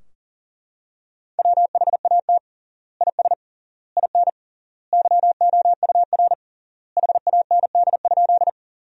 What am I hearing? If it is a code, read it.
Morse code, heard as a keyed tone, about 40 wpm, 700 Hz. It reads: WHAT IS IN YOUR HAND?
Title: 3x Word-Spacing - Sentences from Top 200 Words - 40wpm